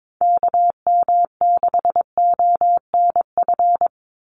Morse code, heard as a keyed tone, about 22 wpm, 700 Hz. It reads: KM6ONF
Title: USA Call Signs 22wpm